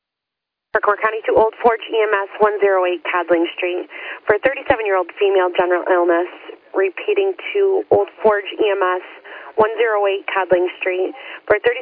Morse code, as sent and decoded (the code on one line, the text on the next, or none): none